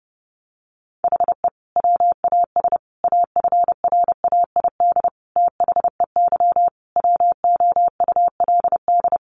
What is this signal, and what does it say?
Morse code, heard as a keyed tone, about 30 wpm, 700 Hz. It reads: HE WAS AFRAID THEY WOULD
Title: Sentences from Top 1500 Words 30wpm